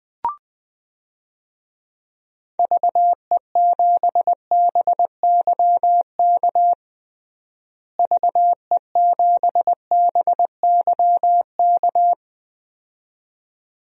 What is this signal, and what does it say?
Morse code, heard as a keyed tone, about 20 wpm, 700 Hz. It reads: VE7BYK VE7BYK
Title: DXCC-100 Call Signs - 6 Characters - Repeated 2x - 20wpm